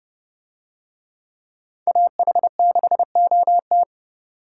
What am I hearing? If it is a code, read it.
Morse code, heard as a keyed tone, about 29 wpm, 700 Hz. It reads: AH6OT